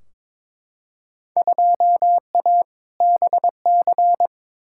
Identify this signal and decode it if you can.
Morse code, heard as a keyed tone, about 21 wpm, 700 Hz. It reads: 2A BC
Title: Field Day Exchanges - 22wpm